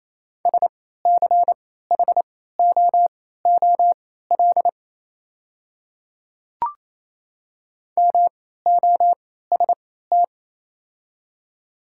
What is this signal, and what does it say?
Morse code, heard as a keyed tone, about 28 wpm, spaced as for 16 wpm, 700 Hz. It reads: SCHOOL MOST
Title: 3x Character Spacing - Top 300 Words - 28wpm